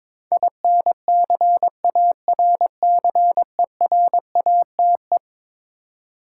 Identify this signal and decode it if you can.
Morse code, heard as a keyed tone, about 21 wpm, 700 Hz. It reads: INCARCERATE
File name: practice / 22wpm / Eleven Letter Words - 22wpm